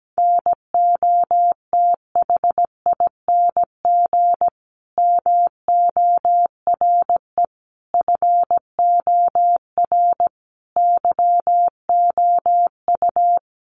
Morse code, heard as a keyed tone, about 17 wpm, 700 Hz. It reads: NOTHING MORE FOR YOU
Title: Sentences from Top 1100 Words 17wpm